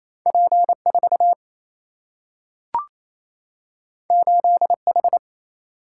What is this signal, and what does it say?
Morse code, heard as a keyed tone, about 28 wpm, 700 Hz. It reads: P 4 8 H
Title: Single-Letter Single-Number 28wpm